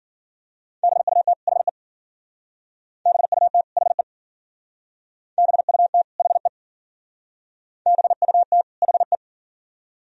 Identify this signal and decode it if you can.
Morse code, heard as a keyed tone, about 51 wpm, 700 Hz. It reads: BUT HE BUT HE BUT HE BUT HE